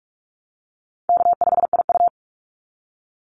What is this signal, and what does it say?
Morse code, heard as a keyed tone, about 44 wpm, 700 Hz. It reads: K5IU
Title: Call Signs Repeated 2x - 45wpm